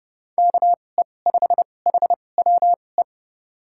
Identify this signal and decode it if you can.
Morse code, heard as a keyed tone, about 30 wpm, spaced as for 22 wpm, 700 Hz. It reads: KE5HWE
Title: USA Call Signs - 6 Characters - 2x Character Spacing - 30wpm